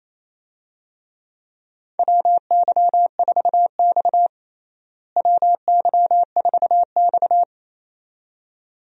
Morse code, heard as a keyed tone, about 28 wpm, 700 Hz. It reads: WY4X WY4X
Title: USA Call Signs - 4 Characters - Repeated 2x - 28wpm